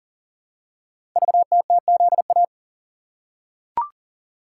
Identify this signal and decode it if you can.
Morse code, heard as a keyed tone, about 40 wpm, 700 Hz. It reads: VTTZA
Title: Code Groups - 5 Letters - 40wpm